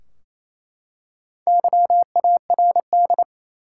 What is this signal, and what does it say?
Morse code, heard as a keyed tone, about 27 wpm, 700 Hz. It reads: YARD
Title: Four Letter Words - 28wpm